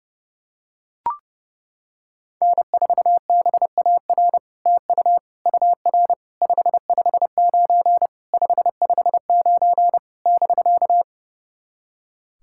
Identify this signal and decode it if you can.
Morse code, heard as a keyed tone, about 30 wpm, 700 Hz. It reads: N4BAR TU UR 559 559 <BK>
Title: SOTA QSO Practice 30wpm